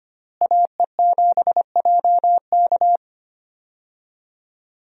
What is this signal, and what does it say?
Morse code, heard as a keyed tone, about 25 wpm, 700 Hz. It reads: AE7JK
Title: USA Call Signs - 5 Characters - Repeated 2x - 25wpm